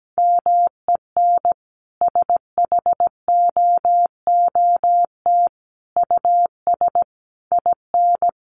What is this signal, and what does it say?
Morse code, heard as a keyed tone, about 17 wpm, 700 Hz. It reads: MEN SHOOT US IN
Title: Sentences from Top 1200 Words 17wpm